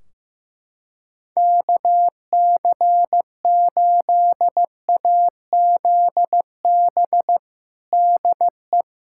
Morse code, heard as a keyed tone, about 16 wpm, 700 Hz. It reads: KC8AZB DE
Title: Practice QSO - Exchange 4 - Ending - 15wpm